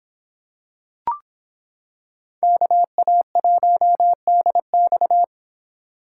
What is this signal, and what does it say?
Morse code, heard as a keyed tone, about 26 wpm, 700 Hz. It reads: KA1DX